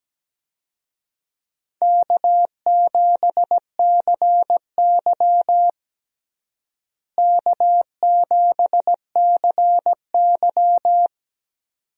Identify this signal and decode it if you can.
Morse code, heard as a keyed tone, about 17 wpm, 700 Hz. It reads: K7CY K7CY
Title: USA Call Signs - 4 Characters - Repeated 3x - 17wpm